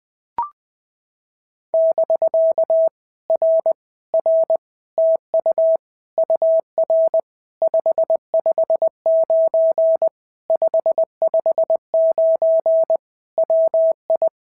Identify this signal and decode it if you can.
Morse code, heard as a keyed tone, about 20 wpm, 650 Hz. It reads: <BK> R R TU UR 559 559 WI
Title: POTA QSO Practice - 20wpm